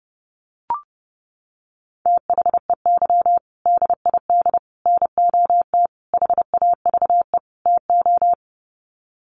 Morse code, heard as a keyed tone, about 30 wpm, 700 Hz. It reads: THEY DID NOT HAVE TO
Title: Sets of 5 Words Encore2 30wpm